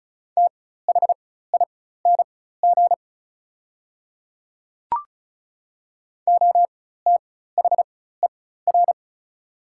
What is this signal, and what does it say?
Morse code, heard as a keyed tone, about 34 wpm, spaced as for 16 wpm, 700 Hz. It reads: THING OTHER
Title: 4x Character Spacing - Top 200 Words - 35wpm